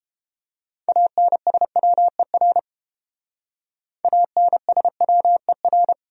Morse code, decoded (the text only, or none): ANSWER ANSWER